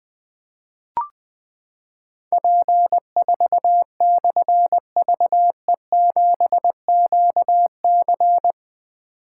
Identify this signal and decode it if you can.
Morse code, heard as a keyed tone, about 20 wpm, 700 Hz. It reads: P4/VE7QC
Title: Difficult Call Signs 20wpm